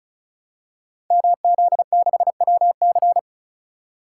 Morse code, heard as a keyed tone, about 36 wpm, 700 Hz. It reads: MZ6WC
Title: Code Groups - 5 Letter:Numbers - 35wpm